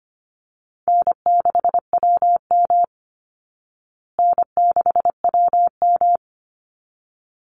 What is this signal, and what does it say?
Morse code, heard as a keyed tone, about 25 wpm, 700 Hz. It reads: N6WM N6WM
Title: Easy Call Signs Repeated 2x - 25wpm